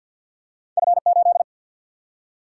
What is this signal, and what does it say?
Morse code, heard as a keyed tone, about 50 wpm, 700 Hz. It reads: F 8
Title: Single-Letter Single-Number 50wpm